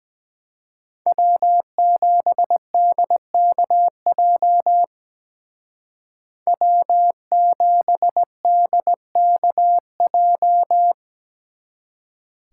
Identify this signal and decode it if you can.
Morse code, heard as a keyed tone, about 20 wpm, 700 Hz. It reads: W7DKJ W7DKJ
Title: Speed Racing - USA 1x3 Call Signs 17wpm